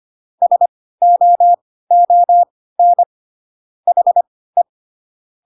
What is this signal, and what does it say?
Morse code, heard as a keyed tone, about 25 wpm, spaced as for 16 wpm, 700 Hz. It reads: SOON HE
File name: Pinocchio / Pinocchio ch5 10 25 ik1hge